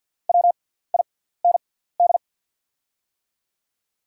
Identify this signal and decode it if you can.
Morse code, heard as a keyed tone, about 50 wpm, spaced as for 17 wpm, 700 Hz. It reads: WIND
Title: 6x Character Spacing - Top 500 Words - 50wpm